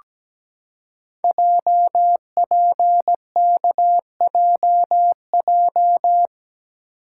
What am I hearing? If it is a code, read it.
Morse code, heard as a keyed tone, about 17 wpm, 700 Hz. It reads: JPKJJ
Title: Code Groups - 5 Letter:Numbers - 17wpm